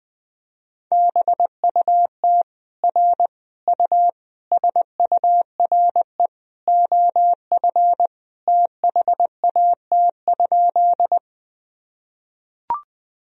Morse code, heard as a keyed tone, about 21 wpm, 700 Hz. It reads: BUT R U SURE OF THAT?
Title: Sentences from Top 300 Words and Abbreviations 20wpm